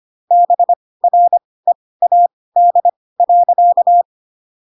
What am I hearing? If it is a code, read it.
Morse code, heard as a keyed tone, about 25 wpm, spaced as for 18 wpm, 700 Hz. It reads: BREAD.